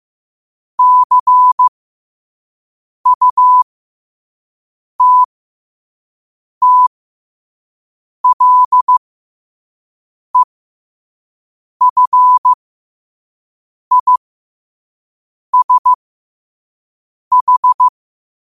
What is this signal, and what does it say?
Morse code, heard as a keyed tone, about 15 wpm, spaced as for 5 wpm, 1000 Hz. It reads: CUTTLEFISH
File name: qotdfiles / 5wpm